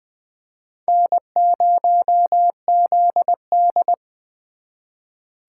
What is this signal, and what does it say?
Morse code, heard as a keyed tone, about 24 wpm, 700 Hz. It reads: N0ZD
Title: Speed Racing - USA 1x2 Call Signs 20wpm